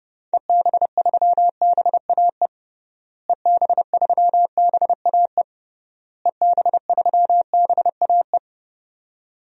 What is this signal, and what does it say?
Morse code, heard as a keyed tone, about 30 wpm, 700 Hz. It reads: EB3BAE EB3BAE EB3BAE
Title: DXCC-100 Call Signs - 6 Characters - Repeated 3x - 30wpm